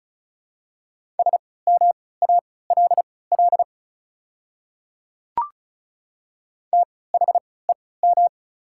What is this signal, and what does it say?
Morse code, heard as a keyed tone, about 36 wpm, spaced as for 20 wpm, 690 Hz. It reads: SMALL THEM
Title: 3x Character Spacing - Top 200 Words - 35wpm